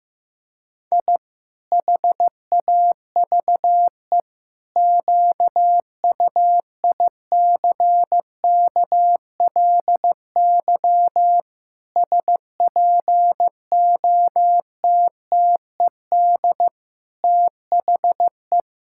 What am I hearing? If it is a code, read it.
Morse code, heard as a keyed tone, about 15 wpm, 700 Hz. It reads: I HAVE QUICKLY SPOTTED THE
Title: Pangram Challenge v2 - 15wpm